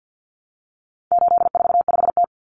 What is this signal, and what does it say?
Morse code, heard as a keyed tone, about 50 wpm, 700 Hz. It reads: 845T
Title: Common Cut Numbers - 4-Digit 50wpm